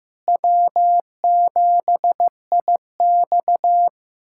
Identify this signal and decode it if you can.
Morse code, heard as a keyed tone, about 15 wpm, 700 Hz. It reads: W7IX
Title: USA Callsign DE Callsign - 4 Characters - 15wpm